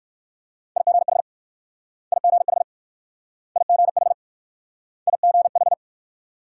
Morse code, heard as a keyed tone, about 56 wpm, 690 Hz. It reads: IGH IGH IGH IGH
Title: Speed-Racing Multi-letter Phonemes - 45wpm